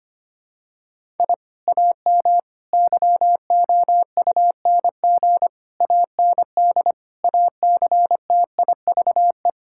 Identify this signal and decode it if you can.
Morse code, heard as a keyed tone, about 25 wpm, 700 Hz. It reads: I AM YOUNG AND ACTIVE
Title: Sentences from Top 1000 Words - Review Part 6 - 25wpm